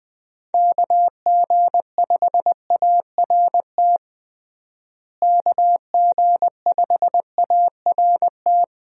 Morse code, heard as a keyed tone, about 20 wpm, 700 Hz. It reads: KG5ART KG5ART
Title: USA Call Signs - 6 Characters - Repeated 2x - 20wpm